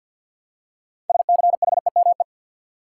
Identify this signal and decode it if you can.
Morse code, heard as a keyed tone, about 52 wpm, 700 Hz. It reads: SCHEME
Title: Six Letter Words - 50wpm